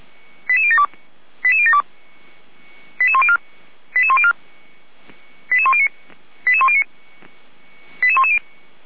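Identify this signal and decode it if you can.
Five-tone selective call (ZVEI), heard as quick runs of five tones, tones of about 70 ms. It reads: 88072 88072 88104 88104 88109 88109 88110
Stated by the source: Five-tone ZVEI selective call received on an SDR.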